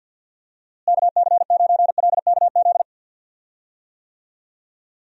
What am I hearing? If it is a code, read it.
Morse code, heard as a keyed tone, about 49 wpm, 700 Hz. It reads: KC9LKZ
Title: USA Call Signs - 6 Characters - Repeated 2x - 50wpm